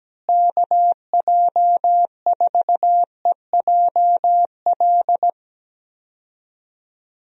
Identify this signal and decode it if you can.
Morse code, heard as a keyed tone, about 17 wpm, 700 Hz. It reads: KJ4EJL
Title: USA Call Signs - 6 Characters - Repeated 2x - 17wpm